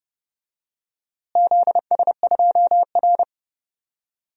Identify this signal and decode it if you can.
Morse code, heard as a keyed tone, about 30 wpm, 700 Hz. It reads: ZS2R
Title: DXCC-100 Call Signs - 4 Characters - Repeated 3x - 30wpm